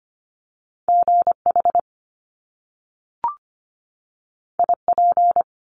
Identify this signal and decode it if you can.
Morse code, heard as a keyed tone, about 25 wpm, 700 Hz. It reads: GH IP
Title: Top Two Letter Combinations in Words 25wpm